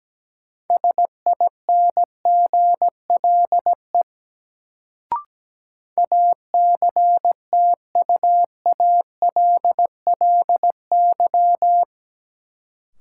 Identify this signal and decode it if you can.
Morse code, heard as a keyed tone, about 17 wpm, 700 Hz. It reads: SINGLE ACTUALLY
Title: Sentences from Top 700 Words - New Word Drill - 17wpm